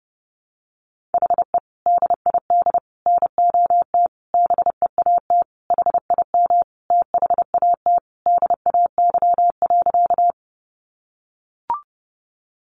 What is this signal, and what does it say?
Morse code, heard as a keyed tone, about 30 wpm, 700 Hz. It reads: HE DID NOT BEAT HIM THAT DAY.
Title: Sentences from Top 1300 Words 30wpm